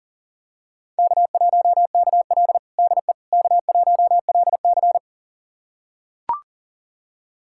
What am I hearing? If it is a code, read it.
Morse code, heard as a keyed tone, about 40 wpm, 700 Hz. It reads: K1KL DE K1LC
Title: USA Callsign DE Callsign - 4 Characters - 40wpm